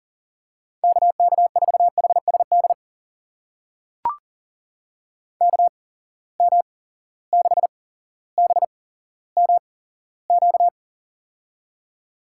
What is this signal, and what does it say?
Morse code, heard as a keyed tone, about 40 wpm, spaced as for 17 wpm, 700 Hz. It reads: KK4HSD KM6BMQ
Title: USA Call Signs - 8x Character Spacing - 40wpm